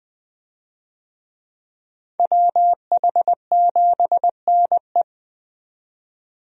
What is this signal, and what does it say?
Morse code, heard as a keyed tone, about 20 wpm, 700 Hz. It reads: WH7NE
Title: USA Call Signs - 5 Characters - 20wpm